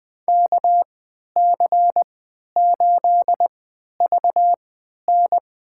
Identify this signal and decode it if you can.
Morse code, heard as a keyed tone, about 20 wpm, spaced as for 11 wpm, 700 Hz. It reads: KC8VN
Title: USA Call Signs - 3x Character Spacing - 20wpm